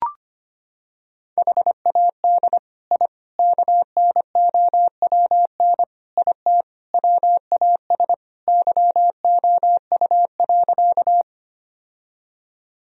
Morse code, heard as a keyed tone, about 25 wpm, 700 Hz. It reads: HAD I KNOWN IT WAS YOU.
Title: Sentences from Top 1000 Words - Review Part 3 - 25wpm